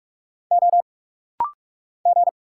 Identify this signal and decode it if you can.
Morse code, heard as a keyed tone, about 46 wpm, 700 Hz. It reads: O G
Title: Single Letters - Rapid-Fire 45wpm